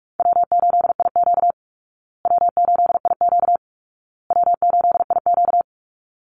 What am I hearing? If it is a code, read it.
Morse code, heard as a keyed tone, about 44 wpm, 700 Hz. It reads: W8IQ W8IQ W8IQ
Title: Call Signs Repeated 2x - 45wpm